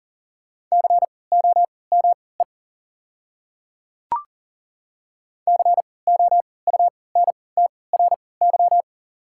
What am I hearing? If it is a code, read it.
Morse code, heard as a keyed tone, about 40 wpm, spaced as for 23 wpm, 700 Hz. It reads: COME COUNTRY
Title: 3x Character Spacing - Top 300 Words - 40wpm